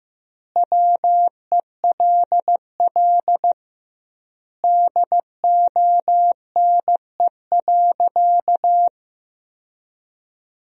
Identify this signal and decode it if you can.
Morse code, heard as a keyed tone, about 15 wpm, 700 Hz. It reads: WELL DONE.
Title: 2x Word-Spacing - Sentences from Top 500 Words - 15wpm